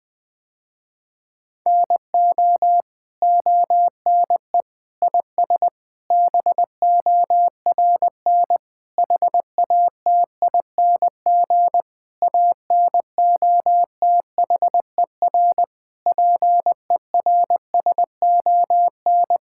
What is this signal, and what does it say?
Morse code, heard as a keyed tone, about 20 wpm, 700 Hz. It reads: NO ONE IS BORN HATING ANOTHER PERSON
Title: Sentences with 9-Word Phrases - Spoken Beginning - 20wpm